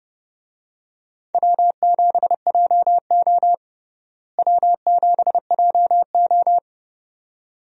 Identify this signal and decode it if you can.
Morse code, heard as a keyed tone, about 30 wpm, 700 Hz. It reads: W7JO W7JO
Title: USA Call Signs - 4 Characters - Repeated 2x - 30wpm